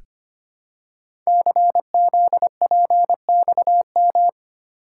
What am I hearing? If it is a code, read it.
Morse code, heard as a keyed tone, about 25 wpm, 700 Hz. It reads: CZPXM
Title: Code Groups - 5 Letters - 25wpm